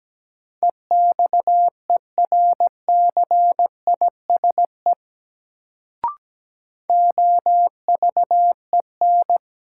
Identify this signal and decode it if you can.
Morse code, heard as a keyed tone, about 17 wpm, 700 Hz. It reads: EXERCISE OVEN
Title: Sentences from Top 1000 Words - Review Part 4 - New Word Drill - 17wpm